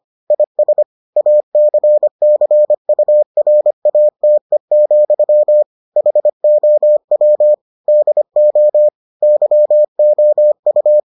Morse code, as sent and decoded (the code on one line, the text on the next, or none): .. ... / .- -.-. -.-. ..- .-. .- - . --..-- / .... --- .-- / -.. --- / -.-- --- ..-
IS ACCURATE, HOW DO YOU